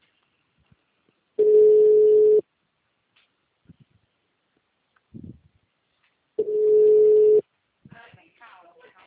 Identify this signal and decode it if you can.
Ringback tone (Europe, 425 Hz), with speech.
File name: out- -2013- - - .21195